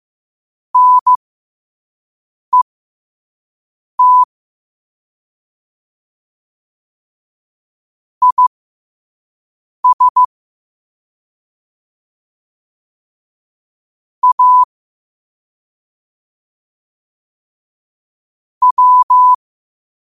Morse code, heard as a keyed tone, about 15 wpm, spaced as for 5 wpm, 1000 Hz. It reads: NET IS A W
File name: qotdfiles / 5wpm